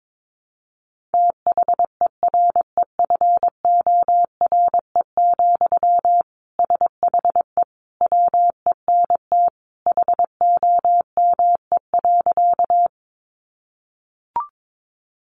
Morse code, heard as a keyed tone, about 22 wpm, 700 Hz. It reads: THEREFORE, SHE WENT HOME.
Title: Sentences from Top 800 Words 22wpm